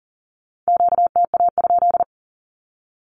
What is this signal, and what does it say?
Morse code, heard as a keyed tone, about 40 wpm, 700 Hz. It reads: QTA?
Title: Q-Codes Repeated 3x 40wpm